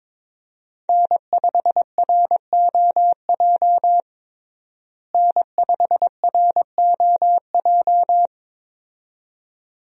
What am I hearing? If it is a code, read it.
Morse code, heard as a keyed tone, about 22 wpm, 700 Hz. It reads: N5ROJ N5ROJ
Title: USA Call Signs - 5 Characters - Repeated 2x - 22wpm